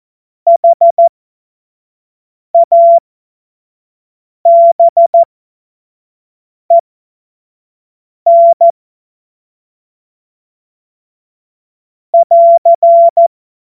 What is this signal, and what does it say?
Morse code, heard as a keyed tone, about 14 wpm, spaced as for 5 wpm, 680 Hz. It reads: HABEN <AR>
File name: Morsekurs / Witze / Witz 20. Dezember 2022 25 F70